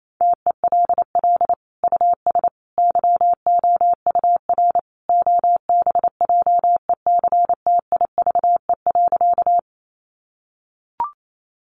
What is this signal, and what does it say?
Morse code, heard as a keyed tone, about 28 wpm, 700 Hz. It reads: TELL US YOUR OBJECTIVE.
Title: Sentences from Top 1200 Words 28wpm